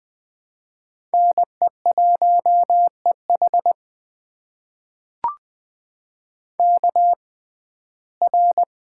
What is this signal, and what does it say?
Morse code, heard as a keyed tone, about 20 wpm, spaced as for 15 wpm, 700 Hz. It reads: NE1EH KR
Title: USA Call Signs - 5 Characters - 6x Character Spacing - 20wpm